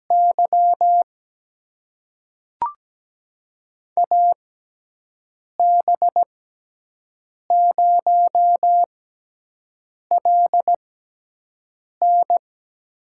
Morse code, heard as a keyed tone, about 17 wpm, spaced as for 6 wpm, 700 Hz. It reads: Y AB0LN
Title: USA Call Signs - 5 Characters - 6x Character Spacing - 17wpm